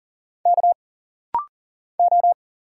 Morse code, heard as a keyed tone, about 40 wpm, 700 Hz. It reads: K O
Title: Single Letters - Rapid-Fire 40wpm